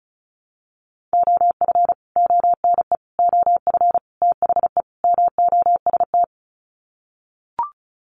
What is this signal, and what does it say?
Morse code, heard as a keyed tone, about 34 wpm, 700 Hz. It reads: OF ONE OF THE MOST
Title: Sets of 5 Words Encore2 35wpm